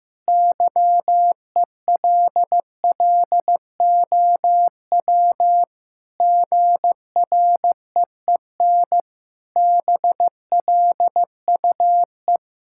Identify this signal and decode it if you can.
Morse code, heard as a keyed tone, about 15 wpm, 700 Hz. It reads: YELLOW GREEN BLUE
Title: Pangram Challenge v2 - 15wpm